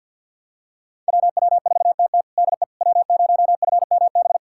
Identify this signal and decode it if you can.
Morse code, heard as a keyed tone, about 46 wpm, 700 Hz. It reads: WW4TT DE W0FMB
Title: USA Callsign DE Callsign - 5 Characters - 50wpm